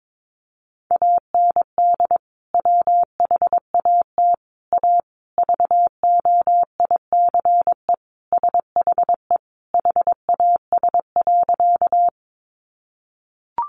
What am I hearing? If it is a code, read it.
Morse code, heard as a keyed tone, about 22 wpm, 700 Hz. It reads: AND WHAT A VOICE SHE HAS.
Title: Sentences from Top 500 Words 22wpm